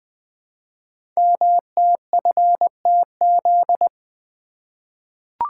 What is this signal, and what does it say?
Morse code, heard as a keyed tone, about 20 wpm, 700 Hz. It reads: MTFTZ
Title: Code Groups - 5 Letter:Numbers - 20wpm